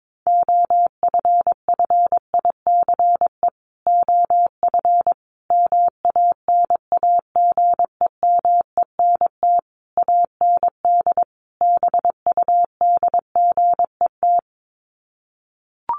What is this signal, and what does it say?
Morse code, heard as a keyed tone, about 22 wpm, 700 Hz. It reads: OFFICE OF MANAGEMENT AND BUDGET
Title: Sets of 5 Words Encore2 22wpm